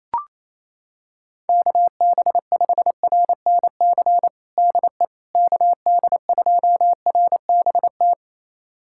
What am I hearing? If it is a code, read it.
Morse code, heard as a keyed tone, about 28 wpm, 700 Hz. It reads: KB5RNC DE KD2RBT
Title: USA Callsign DE Callsign - 6 Characters - 28wpm